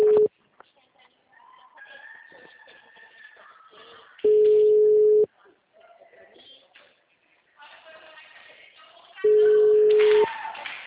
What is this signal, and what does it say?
Ringback tone (Europe, 425 Hz), with speech.